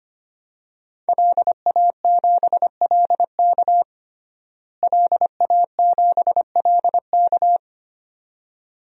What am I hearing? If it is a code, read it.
Morse code, heard as a keyed tone, about 25 wpm, 700 Hz. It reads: LA7LK LA7LK
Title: DXCC-100 Call Signs - 5 Characters - Repeated 2x - 25wpm